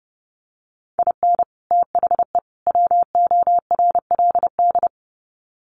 Morse code, heard as a keyed tone, about 30 wpm, 700 Hz. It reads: IN THE WORLD
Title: Sets of 3 Words 30wpm